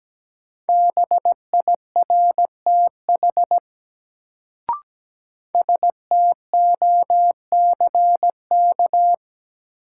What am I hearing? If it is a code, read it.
Morse code, heard as a keyed tone, about 17 wpm, 700 Hz. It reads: BIRTH STOCK
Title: Sentences from Top 800 Words - New Word Drill - 17wpm